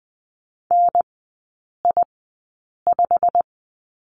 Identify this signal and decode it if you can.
Morse code, heard as a keyed tone, about 20 wpm, spaced as for 8 wpm, 700 Hz. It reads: NI5
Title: 087 Lesson 15e - Number 5 - Three Characters 20wpm